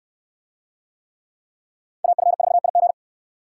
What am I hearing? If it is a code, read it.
Morse code, heard as a keyed tone, about 69 wpm, 700 Hz. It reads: AF4IG